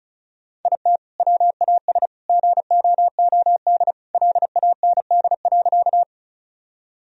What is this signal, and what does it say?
Morse code, heard as a keyed tone, about 36 wpm, 700 Hz. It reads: IT WAS GOOD LAND.
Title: Sentences from Top 1000 Words - Review Part 1 - 35wpm